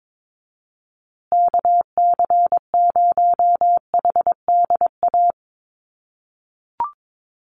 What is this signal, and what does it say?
Morse code, heard as a keyed tone, about 22 wpm, 700 Hz. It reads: KC0HDA
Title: USA Call Signs Repeated 2x 22wpm